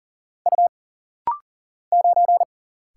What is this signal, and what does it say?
Morse code, heard as a keyed tone, about 40 wpm, 700 Hz. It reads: U 9
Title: Single Letter-Number - Rapid-Fire 40wpm